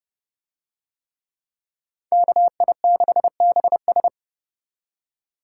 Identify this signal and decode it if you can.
Morse code, heard as a keyed tone, about 30 wpm, 700 Hz. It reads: KI6BS